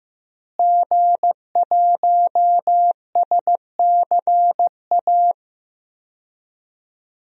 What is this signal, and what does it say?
Morse code, heard as a keyed tone, about 15 wpm, 700 Hz. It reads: G1SCA
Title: DXCC-100 Call Signs - 5 Characters - Repeated 3x - 15wpm